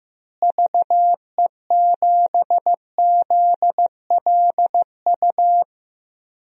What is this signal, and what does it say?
Morse code, heard as a keyed tone, about 16 wpm, 700 Hz. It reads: VE7ZLU
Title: DXCC-100 Call Signs - 6 Characters - Repeated 3x - 15wpm